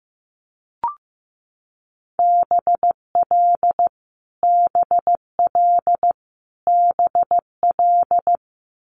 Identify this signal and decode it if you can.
Morse code, heard as a keyed tone, about 15 wpm, 700 Hz. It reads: BL BL BL
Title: Multi-letter Phonemes Repeated 3x 15wpm